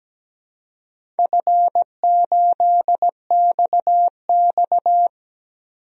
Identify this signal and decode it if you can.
Morse code, heard as a keyed tone, about 17 wpm, 700 Hz. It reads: F8XX
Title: DXCC-100 Call Signs - 4 Characters - 17wpm